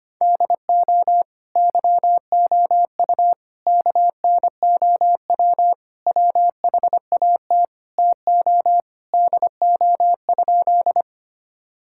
Morse code, heard as a keyed tone, about 25 wpm, 700 Hz. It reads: DO YOU KNOW WHAT TO DO?
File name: practice / 25wpm / Sentences with 2-Word Phrases - 25wpm